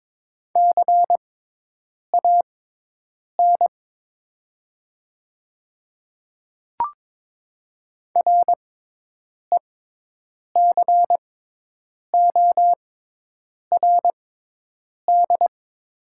Morse code, heard as a keyed tone, about 22 wpm, spaced as for 8 wpm, 700 Hz. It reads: CAN RECORD
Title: 6x Character Spacing - Top 500 Words - 22wpm